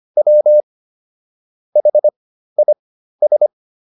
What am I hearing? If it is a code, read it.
Morse code, heard as a keyed tone, about 25 wpm, spaced as for 13 wpm, 600 Hz. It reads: W HIS